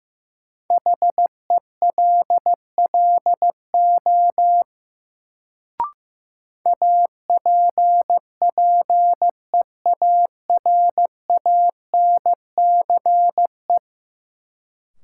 Morse code, heard as a keyed tone, about 15 wpm, 700 Hz. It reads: HELLO APPEARANCE
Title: Sentences from Top 1100 Words - New Word Drill - 15wpm